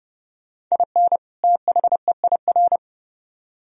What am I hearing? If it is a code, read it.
Morse code, heard as a keyed tone, about 38 wpm, 700 Hz. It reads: IN THEIR